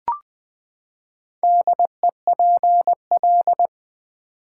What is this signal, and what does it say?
Morse code, heard as a keyed tone, about 20 wpm, 700 Hz. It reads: DEPL
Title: Build A Word - 4 Letters At A Time - 10-Letter Words - 20wpm